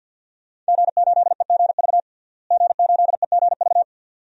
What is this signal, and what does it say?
Morse code, heard as a keyed tone, about 50 wpm, 700 Hz. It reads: G8EGV G8EGV
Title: DXCC-100 Call Signs - 5 Characters - Repeated 2x - 50wpm